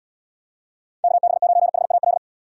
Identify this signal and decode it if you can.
Morse code, heard as a keyed tone, about 78 wpm, 700 Hz. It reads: KB9HAL